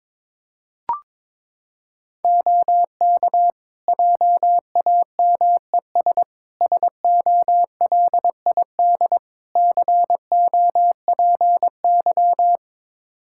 Morse code, heard as a keyed tone, about 22 wpm, 700 Hz. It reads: OK JAMES SOLID COPY
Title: Practice QSO - Exchange 3 - More Info - 22wpm